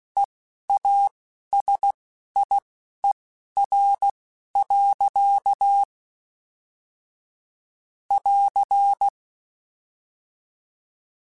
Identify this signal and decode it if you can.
Morse code, heard as a keyed tone, about 16 wpm, spaced as for 11 wpm, 800 Hz. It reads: EASIER. <AR>
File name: Technical Subjects / Technical Subjects-mp3s / AND Gate @10